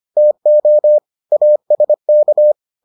Morse code, heard as a keyed tone, about 25 wpm, 600 Hz. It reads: TO ASK